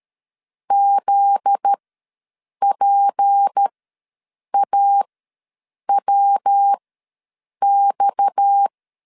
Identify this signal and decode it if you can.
Morse code, heard as a keyed tone, about 13 wpm, spaced as for 7 wpm, 800 Hz. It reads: ZPAWX
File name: 7 GF9Z ws 12wpm